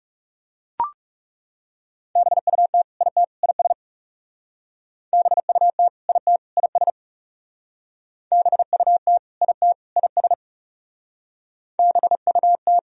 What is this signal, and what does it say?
Morse code, heard as a keyed tone, about 40 wpm, 700 Hz. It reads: BUT IT IS BUT IT IS BUT IT IS BUT